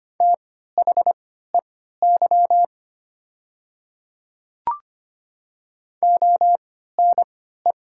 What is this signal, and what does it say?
Morse code, heard as a keyed tone, about 25 wpm, spaced as for 14 wpm, 690 Hz. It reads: THEY ONE
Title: 3x Character Spacing - Top 100 Words - 25wpm